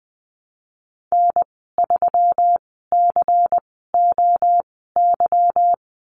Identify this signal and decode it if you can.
Morse code, heard as a keyed tone, about 20 wpm, spaced as for 14 wpm, 700 Hz. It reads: N3COY
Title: 154 Lesson 25g - Letter p - Call Signs 20wpm